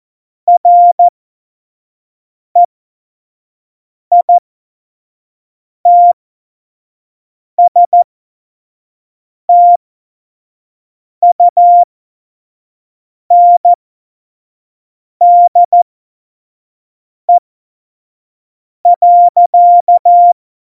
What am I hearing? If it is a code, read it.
Morse code, heard as a keyed tone, about 14 wpm, spaced as for 5 wpm, 700 Hz. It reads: REITSTUNDE.
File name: Morsekurs / Witze / Witz 18 August 2020, 25 F70